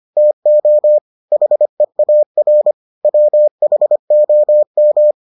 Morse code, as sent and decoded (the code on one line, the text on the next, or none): - --- / .... .- .-. / .-- .... --- --
TO HEAR WHOM